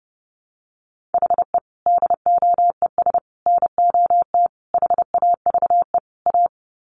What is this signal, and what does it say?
Morse code, heard as a keyed tone, about 29 wpm, 700 Hz. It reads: HE DOES NOT HAVE A
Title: Sets of 5 Words Encore 30wpm